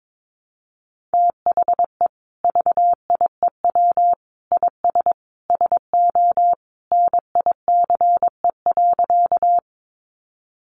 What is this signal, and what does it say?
Morse code, heard as a keyed tone, about 22 wpm, 700 Hz. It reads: THE VIEW IS SO NICE.
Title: Additional Easy Phrases 22wpm